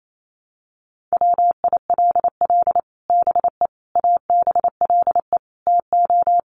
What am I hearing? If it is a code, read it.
Morse code, heard as a keyed tone, about 28 wpm, 700 Hz. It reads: WILL BE ABLE TO
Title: Sets of 4 Words 28wpm